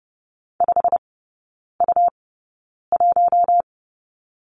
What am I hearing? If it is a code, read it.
Morse code, heard as a keyed tone, about 30 wpm, spaced as for 9 wpm, 700 Hz. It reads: 5U1
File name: Lesson 17e - Letter u - Three Characters 30wpm